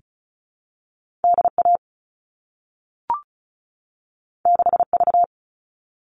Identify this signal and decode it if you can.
Morse code, heard as a keyed tone, about 36 wpm, 700 Hz. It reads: DA 6V
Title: All Cut Numbers - 2-Digit 35wpm